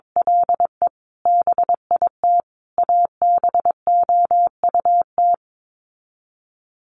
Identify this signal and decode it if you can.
Morse code, heard as a keyed tone, about 22 wpm, 700 Hz. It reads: LE BIT ABOUT
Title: Sets of 5 Words Encore 22wpm